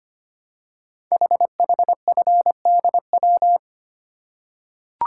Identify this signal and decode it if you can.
Morse code, heard as a keyed tone, about 25 wpm, 700 Hz. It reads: HHFDW
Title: Code Groups - 5 Letters - 25wpm